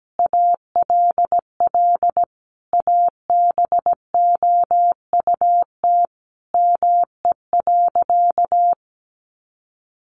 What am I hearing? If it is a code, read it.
Morse code, heard as a keyed tone, about 17 wpm, 700 Hz. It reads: ALL ABOUT ME.
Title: Sentences from Top 1200 Words 17wpm